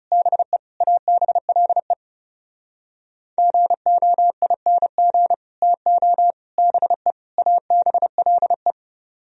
Morse code, heard as a keyed tone, about 30 wpm, 700 Hz. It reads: BE ABLE GOING TO BE ABLE
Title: Speed-Racing - Sets of 4 Words 30wpm